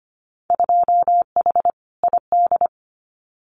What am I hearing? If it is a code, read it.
Morse code, heard as a keyed tone, about 25 wpm, 700 Hz. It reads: 2H ID
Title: Winter Field Day Exchanges 25wpm